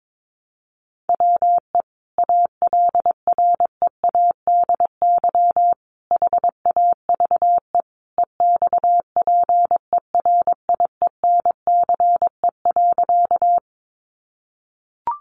Morse code, heard as a keyed tone, about 22 wpm, 700 Hz. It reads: WE ALREADY HAVE EXPERIENCE.
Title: Sentences from Top 700 Words 22wpm